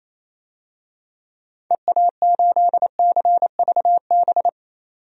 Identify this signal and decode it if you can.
Morse code, heard as a keyed tone, about 28 wpm, 700 Hz. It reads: EA8CVB
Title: DXCC-100 Call Signs - 6 Characters - 28wpm